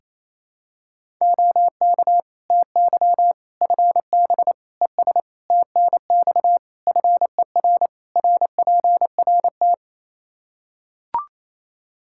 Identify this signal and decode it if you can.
Morse code, heard as a keyed tone, about 28 wpm, 700 Hz. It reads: OK TY FB ES TNX FER RPRT
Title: Practice QSO - Exchange 2 - Info - 28wpm